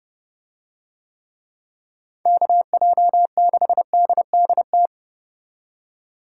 Morse code, heard as a keyed tone, about 31 wpm, 700 Hz. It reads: KJ6DDT